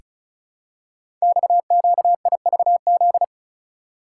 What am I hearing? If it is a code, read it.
Morse code, heard as a keyed tone, about 36 wpm, 690 Hz. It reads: XQIVZ
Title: Code Groups - 5 Letters - 35wpm